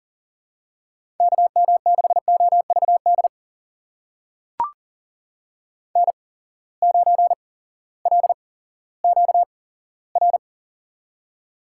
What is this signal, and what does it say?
Morse code, heard as a keyed tone, about 38 wpm, spaced as for 18 wpm, 700 Hz. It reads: KM6OVD N9LQR
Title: USA Call Signs - 8x Character Spacing - 40wpm